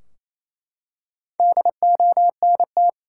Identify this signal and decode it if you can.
Morse code, heard as a keyed tone, about 28 wpm, 700 Hz. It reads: DONT
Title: Top 180 QSO Elements 28wpm